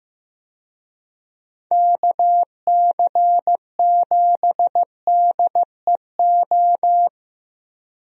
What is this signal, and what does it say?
Morse code, heard as a keyed tone, about 15 wpm, 700 Hz. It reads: KC7DEO
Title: USA Call Signs - 6 Characters - Repeated 2x - 15wpm